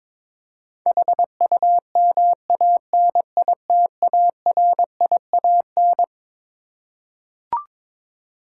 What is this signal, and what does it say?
Morse code, heard as a keyed tone, about 22 wpm, 700 Hz. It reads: HUMANITARIAN
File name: Twelve Letter Words - 22wpm